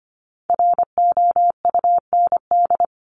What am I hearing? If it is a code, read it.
Morse code, heard as a keyed tone, about 25 wpm, 700 Hz. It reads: ROUND
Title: Top 200 Words Repeated 25wpm